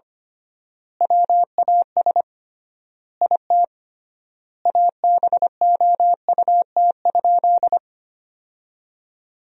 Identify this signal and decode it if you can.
Morse code, heard as a keyed tone, about 25 wpm, 700 Hz. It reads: WAS IT ABOUT?
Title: 3x Word-Spacing - Sentences from Top 100 Words - 25wpm